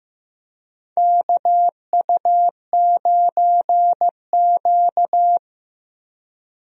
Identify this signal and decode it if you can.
Morse code, heard as a keyed tone, about 15 wpm, 700 Hz. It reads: KU9Q
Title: USA Call Signs - 4 Characters - Repeated 2x - 15wpm